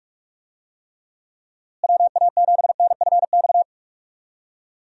Morse code, heard as a keyed tone, about 43 wpm, 700 Hz. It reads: WA7NFX
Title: USA Call Signs - 6 Characters - Repeated 2x - 45wpm